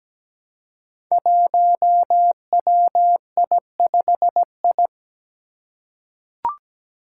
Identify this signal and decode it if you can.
Morse code, heard as a keyed tone, about 17 wpm, 700 Hz. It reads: 1WI5I
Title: Five Challenging Letter-Number Combinations 17wpm